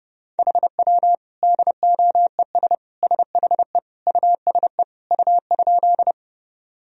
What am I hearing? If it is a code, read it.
Morse code, heard as a keyed tone, about 30 wpm, 700 Hz. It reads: HW DOES SHE USE U?
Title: Sentences from Top 200 Words and Abbreviations 30wpm